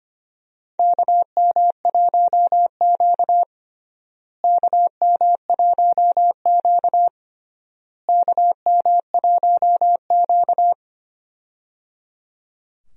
Morse code, heard as a keyed tone, about 25 wpm, 700 Hz. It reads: KM1Q KM1Q KM1Q
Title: USA Call Signs - 4 Characters - Repeated 3x - 25wpm